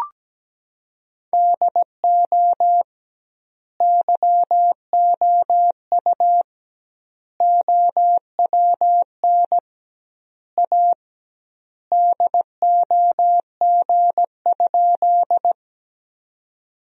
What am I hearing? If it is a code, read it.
Morse code, heard as a keyed tone, about 17 wpm, 700 Hz. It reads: DO YOU OWN A DOG?
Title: 2x Word-Spacing - Sentences from Top 400 Words - 17wpm